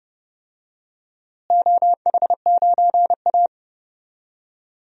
Morse code, heard as a keyed tone, about 31 wpm, 700 Hz. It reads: OH9A
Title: DXCC-100 Call Signs - 4 Characters - Repeated 2x - 30wpm